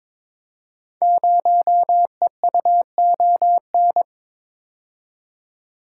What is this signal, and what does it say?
Morse code, heard as a keyed tone, about 22 wpm, 700 Hz. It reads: 0EUON
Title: Code Groups - 5 Letter:Numbers - 22wpm